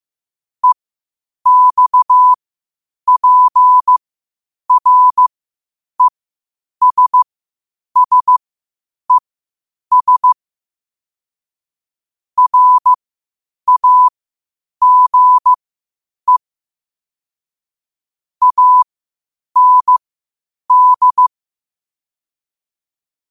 Morse code, heard as a keyed tone, about 15 wpm, spaced as for 8 wpm, 1000 Hz. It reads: EXPRESSES RAGE AND
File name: qotdfiles / 8wpm